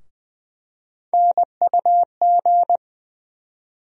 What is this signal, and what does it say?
Morse code, heard as a keyed tone, about 20 wpm, 700 Hz. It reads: NUG
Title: USA Call Sign Postfixes - 3 Letters 20wpm